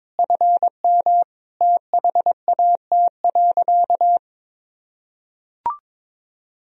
Morse code, heard as a keyed tone, about 22 wpm, 700 Hz. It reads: FM THAT.
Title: Sentences from Top 300 Words and Abbreviations 22wpm